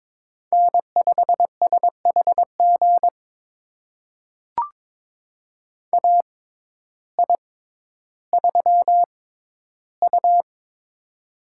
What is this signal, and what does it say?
Morse code, heard as a keyed tone, about 21 wpm, spaced as for 12 wpm, 700 Hz. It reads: N5SHG AI3U
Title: USA Call Signs - 6x Character Spacing - 22wpm